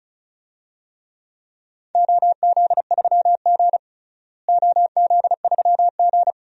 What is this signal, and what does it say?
Morse code, heard as a keyed tone, about 34 wpm, 700 Hz. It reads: OZ3G OZ3G
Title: DXCC-100 Call Signs - 4 Characters - Repeated 2x - 35wpm